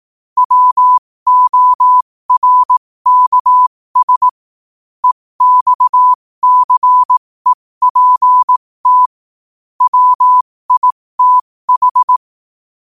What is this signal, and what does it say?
Morse code, heard as a keyed tone, about 18 wpm, spaced as for 15 wpm, 1000 Hz. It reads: WORKS EXCEPT WITH
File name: qotdfiles / 15wpm